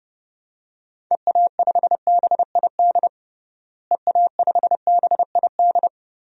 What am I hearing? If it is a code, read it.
Morse code, heard as a keyed tone, about 30 wpm, 700 Hz. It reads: EA5BID EA5BID
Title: DXCC-100 Call Signs - 6 Characters - Repeated 2x - 30wpm